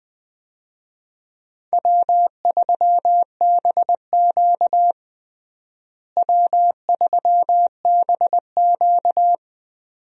Morse code, heard as a keyed tone, about 20 wpm, 700 Hz. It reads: W3BQ W3BQ
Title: USA Call Signs - 4 Characters - Repeated 2x - 20wpm